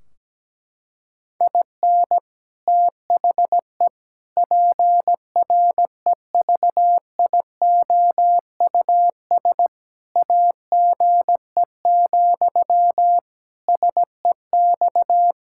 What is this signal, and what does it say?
Morse code, heard as a keyed tone, about 17 wpm, 700 Hz. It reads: IN THE PREVIOUS AGE, SEX
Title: Sentences from Top 1600 Words 17wpm